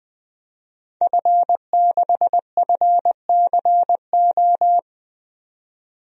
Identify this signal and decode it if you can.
Morse code, heard as a keyed tone, about 20 wpm, 700 Hz. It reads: F6FCO
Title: DXCC-100 Call Signs - 5 Characters - 20wpm